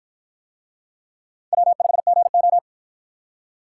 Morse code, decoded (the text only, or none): W5GO